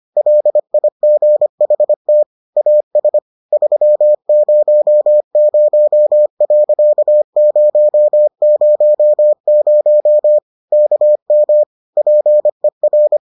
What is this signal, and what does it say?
Morse code, heard as a keyed tone, about 25 wpm, 600 Hz. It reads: LIGHT AS 300.000 KM PER